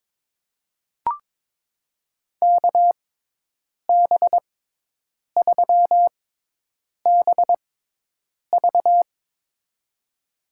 Morse code, heard as a keyed tone, about 21 wpm, spaced as for 8 wpm, 700 Hz. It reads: KB3BV
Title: USA Call Signs - 6x Character Spacing - 22wpm